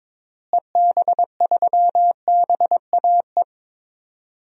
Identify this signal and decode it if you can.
Morse code, heard as a keyed tone, about 22 wpm, 700 Hz. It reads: EB3BAE
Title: DXCC-100 Call Signs - 6 Characters - Repeated 3x - 22wpm